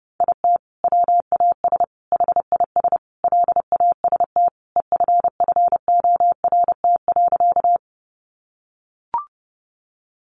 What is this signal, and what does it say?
Morse code, heard as a keyed tone, about 30 wpm, 700 Hz. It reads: IT WAS HIS LAST EFFORT.
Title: Sentences from Top 1200 Words 30wpm